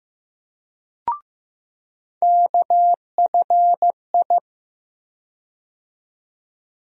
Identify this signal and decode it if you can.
Morse code, heard as a keyed tone, about 16 wpm, 700 Hz. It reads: KFI
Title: USA Call Sign Postfixes - 3 Letters 15wpm